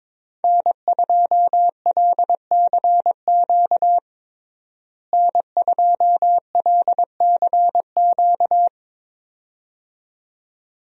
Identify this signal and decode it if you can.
Morse code, heard as a keyed tone, about 21 wpm, 700 Hz. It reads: N2LCQ N2LCQ
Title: USA Call Signs - 5 Characters - Repeated 2x - 22wpm